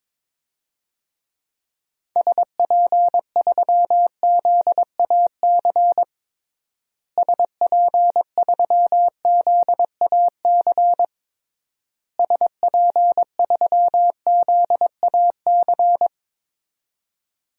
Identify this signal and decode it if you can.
Morse code, heard as a keyed tone, about 22 wpm, 700 Hz. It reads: SP3ZAC SP3ZAC SP3ZAC
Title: DXCC-100 Call Signs - 6 Characters - Repeated 3x - 22wpm